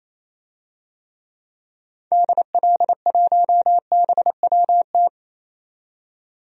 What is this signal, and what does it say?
Morse code, heard as a keyed tone, about 28 wpm, 700 Hz. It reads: DL1BWT
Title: DXCC-100 Call Signs - 6 Characters - 28wpm